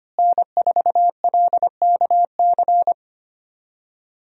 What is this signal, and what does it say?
Morse code, heard as a keyed tone, about 25 wpm, 700 Hz. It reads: N4LKC
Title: USA Call Signs - 5 Characters - Repeated 2x - 25wpm